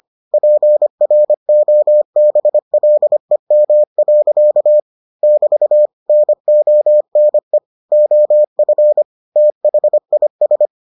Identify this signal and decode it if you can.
Morse code, heard as a keyed tone, about 25 wpm, 600 Hz. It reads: PROBLEM. <BT> NONE OF THIS